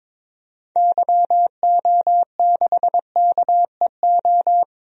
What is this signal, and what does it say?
Morse code, heard as a keyed tone, about 22 wpm, 700 Hz. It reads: YO6KEO
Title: DXCC-100 Call Signs - 6 Characters - 22wpm